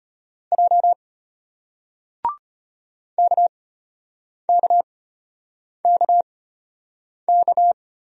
Morse code, heard as a keyed tone, about 34 wpm, spaced as for 8 wpm, 700 Hz. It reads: J KKKK